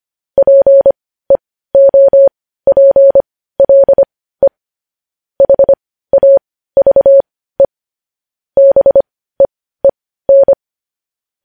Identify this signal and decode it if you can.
Morse code, heard as a keyed tone, about 25 wpm, spaced as for 15 wpm, 550 Hz. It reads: PEOPLE HAVE BEEN
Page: News Headlines in Morse Code at 15 WPM